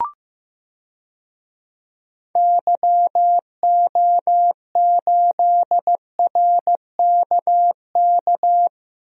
Morse code, heard as a keyed tone, about 15 wpm, 700 Hz. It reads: YO8RKK
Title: DXCC-100 Call Signs - 6 Characters - 15wpm